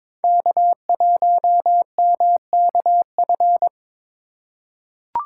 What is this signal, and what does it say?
Morse code, heard as a keyed tone, about 22 wpm, 700 Hz. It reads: K1MKF
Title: USA Call Signs - 5 Characters - 2x Character Spacing - 22wpm